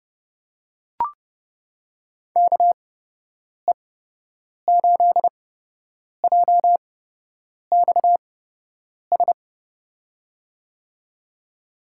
Morse code, heard as a keyed tone, about 30 wpm, spaced as for 8 wpm, 700 Hz. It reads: KE8JXS
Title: USA Call Signs - 8x Character Spacing - 30wpm